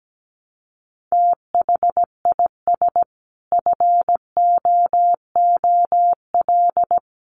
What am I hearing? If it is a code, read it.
Morse code, heard as a keyed tone, about 17 wpm, 700 Hz. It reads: THIS FOOL
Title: A Sound Of Thunder 17wpm